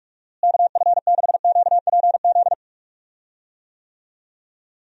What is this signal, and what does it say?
Morse code, heard as a keyed tone, about 46 wpm, 700 Hz. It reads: KF6QPZ
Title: USA Call Signs - 6 Characters - Repeated 3x - 45wpm